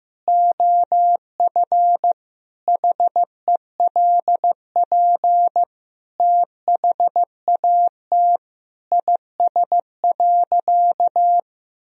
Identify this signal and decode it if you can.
Morse code, heard as a keyed tone, about 15 wpm, 700 Hz. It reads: OF HELP THAT IS.
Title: Sentences from Top 1000 Words - Review Part 4 - 15wpm